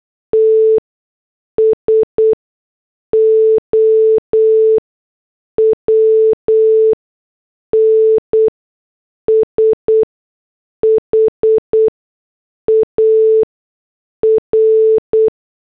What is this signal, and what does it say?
Morse code, heard as a keyed tone, about 8 wpm, spaced as for 6 wpm, 440 Hz. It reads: TS OWN SHAR